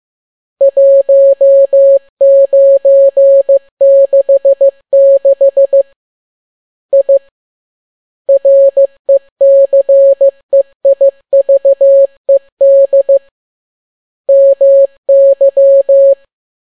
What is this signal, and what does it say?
Morse code, heard as a keyed tone, about 15 wpm, 550 Hz. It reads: 1966 I RECEIVED MY
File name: n6tt p1 15C13